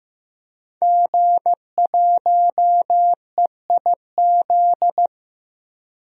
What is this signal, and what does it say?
Morse code, heard as a keyed tone, about 16 wpm, 700 Hz. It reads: G1EIZ
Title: DXCC-100 Call Signs - 5 Characters - 15wpm